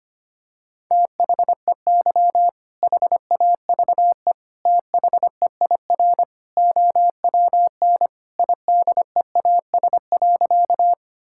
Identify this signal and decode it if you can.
Morse code, heard as a keyed tone, about 25 wpm, 700 Hz. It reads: THEY HAVE THEIR OWN IDEAS.
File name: practice / 25wpm / Sentences with 2-Word Phrases - 25wpm